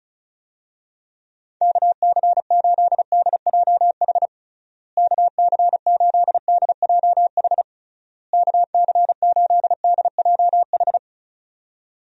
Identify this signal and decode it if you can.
Morse code, heard as a keyed tone, about 34 wpm, 700 Hz. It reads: KC8DJH KC8DJH KC8DJH
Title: USA Call Signs - 6 Characters - Repeated 3x - 35wpm